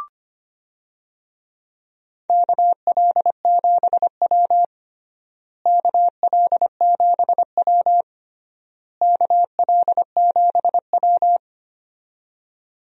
Morse code, heard as a keyed tone, about 25 wpm, 700 Hz. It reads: KL7W KL7W KL7W
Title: DXCC-100 Call Signs - 4 Characters - Repeated 3x - 25wpm